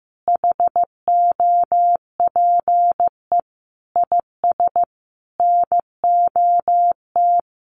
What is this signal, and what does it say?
Morse code, heard as a keyed tone, about 15 wpm, 700 Hz. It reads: HOPE IS NOT
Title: Sentences from Top 1400 Words 15wpm